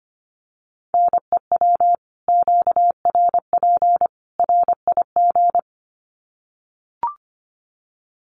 Morse code, heard as a keyed tone, about 25 wpm, 700 Hz. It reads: NEW QRP RIG
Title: Common QSO Phrases 25wpm